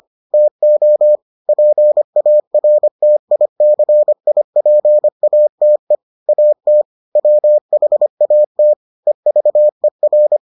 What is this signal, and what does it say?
Morse code, heard as a keyed tone, about 25 wpm, 600 Hz. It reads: TO PARTICIPATE AT WHATEVER